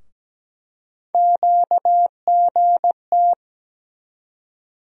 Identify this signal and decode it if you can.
Morse code, heard as a keyed tone, about 17 wpm, 700 Hz. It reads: QGT
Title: USA Call Sign Postfixes - 3 Letters 17wpm